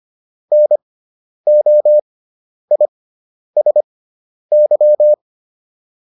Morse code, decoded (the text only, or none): NOISY